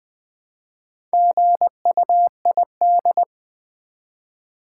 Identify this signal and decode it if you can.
Morse code, heard as a keyed tone, about 20 wpm, 700 Hz. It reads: GUID
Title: Build A Word - 2 Letters At A Time - 8-Letter Words - 20wpm